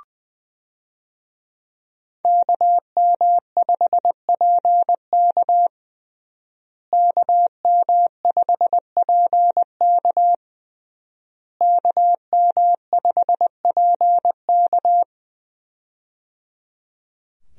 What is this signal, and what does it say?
Morse code, heard as a keyed tone, about 20 wpm, 700 Hz. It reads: KM5PK KM5PK KM5PK
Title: USA Call Signs - 5 Characters - Repeated 3x - 20wpm